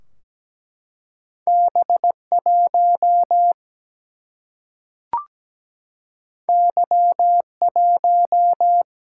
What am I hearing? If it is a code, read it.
Morse code, heard as a keyed tone, about 17 wpm, 700 Hz. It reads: B1 Y1
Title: Single-Letter Single-Number 17wpm